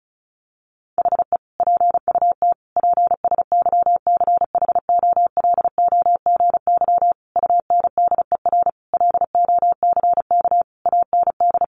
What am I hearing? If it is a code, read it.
Morse code, heard as a keyed tone, about 34 wpm, 700 Hz. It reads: HE PUT PSYCHOLOGY UNDER LOCK AND
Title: Sentences from Top 1400 Words 35wpm